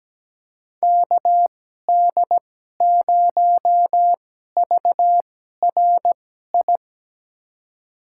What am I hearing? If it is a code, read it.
Morse code, heard as a keyed tone, about 17 wpm, spaced as for 12 wpm, 700 Hz. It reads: KD0VRI
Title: USA Call Signs - 6 Characters - 2x Character Spacing - 17wpm